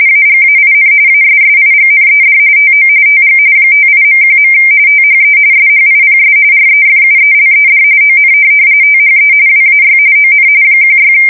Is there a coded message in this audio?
Yes: RTTY from 0.1 to 11.1 s, about 45.45 baud. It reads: AGAIN ASSIGNED TO TG 58.1. THE SHIPS REFUELED ON THE WAY ON 16